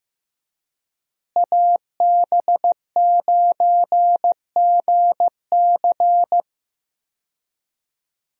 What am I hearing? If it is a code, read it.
Morse code, heard as a keyed tone, about 16 wpm, 700 Hz. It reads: AB9GC
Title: USA Call Signs - 5 Characters - Repeated 2x - 15wpm